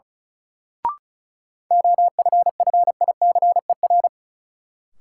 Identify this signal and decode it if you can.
Morse code, heard as a keyed tone, about 34 wpm, 700 Hz. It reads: OFFICER
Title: Sentences from Top 1200 Words - New Word Drill - 35wpm